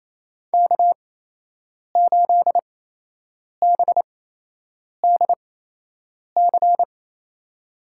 Morse code, heard as a keyed tone, about 28 wpm, spaced as for 8 wpm, 700 Hz. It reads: K8BDC
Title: USA Call Signs - 8x Character Spacing - 28wpm